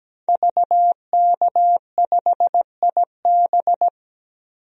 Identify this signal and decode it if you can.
Morse code, heard as a keyed tone, about 17 wpm, 700 Hz. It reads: VK5IB
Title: DXCC-100 Call Signs - 5 Characters - 17wpm